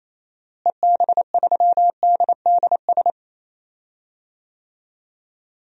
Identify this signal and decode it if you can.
Morse code, heard as a keyed tone, about 28 wpm, 700 Hz. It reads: EB3DDS
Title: DXCC-100 Call Signs - 6 Characters - Repeated 2x - 28wpm